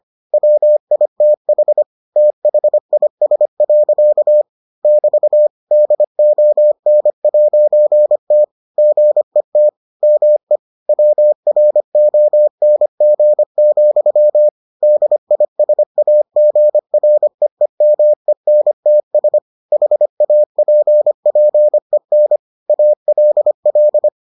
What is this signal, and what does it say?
Morse code, heard as a keyed tone, about 25 wpm, 600 Hz. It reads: WITH THIS. <BT> DON'T GET ME WRONG, DISAGREEMENTS HAPPEN ALL